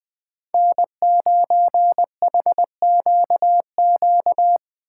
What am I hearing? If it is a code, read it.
Morse code, heard as a keyed tone, about 23 wpm, 700 Hz. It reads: N9HQQ
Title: Speed Racing - USA 1x3 Call Signs 20wpm